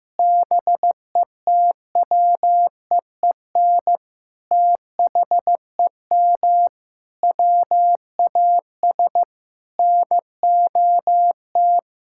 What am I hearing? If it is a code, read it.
Morse code, heard as a keyed tone, about 15 wpm, 700 Hz. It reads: BETWEEN THEM WAS NOT
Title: Sentences from Top 1000 Words - Review Part 6 - 15wpm